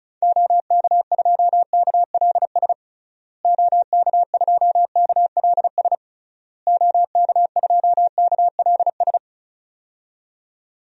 Morse code, heard as a keyed tone, about 36 wpm, 700 Hz. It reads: OK2KLS OK2KLS OK2KLS
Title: DXCC-100 Call Signs - 6 Characters - Repeated 3x - 35wpm